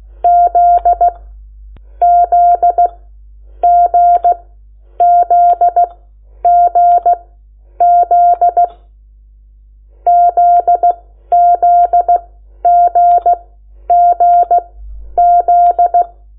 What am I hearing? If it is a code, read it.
Morse code, heard as a keyed tone, about 16 wpm, spaced as for 10 wpm, 700 Hz. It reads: Z Z GZGZ ZZGGZ
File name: Lecciones / CLASE Nº8